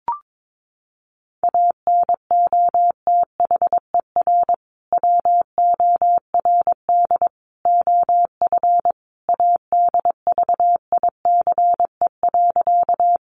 Morse code, heard as a keyed tone, about 22 wpm, 700 Hz. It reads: ANOTHER WORD OF ADVICE.
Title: Sentences from Top 1100 Words 22wpm